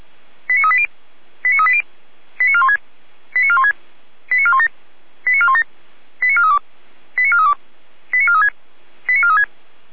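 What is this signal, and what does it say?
Five-tone selective call (ZVEI), heard as quick runs of five tones, tones of about 70 ms. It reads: 89380 89380 89416 89416 89417 89417 89432 89432 89436 89436
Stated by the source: ZVEI five-tone selective call received via SDR, Feuerwehr Orlamünde.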